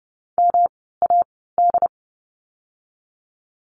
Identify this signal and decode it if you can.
Morse code, heard as a keyed tone, about 30 wpm, spaced as for 17 wpm, 700 Hz. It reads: MAD
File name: course / 30:10wpm / 108 Lesson 18f - Letter m - Words 30wpm